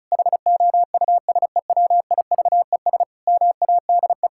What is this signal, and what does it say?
Morse code, heard as a keyed tone, about 34 wpm, 700 Hz. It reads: HOUSEWIVES MADE
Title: Pangram Challenge v2 - 35wpm